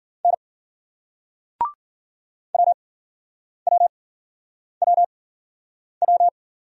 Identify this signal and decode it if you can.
Morse code, heard as a keyed tone, about 52 wpm, 700 Hz. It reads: N W W W W